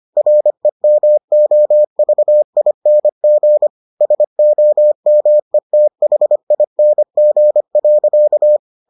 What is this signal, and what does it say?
Morse code, heard as a keyed tone, about 25 wpm, 600 Hz. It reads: REMOVING SOMETHING.